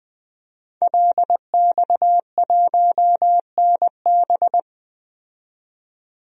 Morse code, heard as a keyed tone, about 21 wpm, 700 Hz. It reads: LX1NB
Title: DXCC-100 Call Signs - 5 Characters - 20wpm